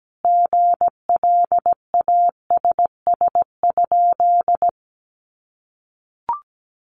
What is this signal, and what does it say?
Morse code, heard as a keyed tone, about 17 wpm, 700 Hz. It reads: GLASS?
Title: Sentences from Top 1300 Words 17wpm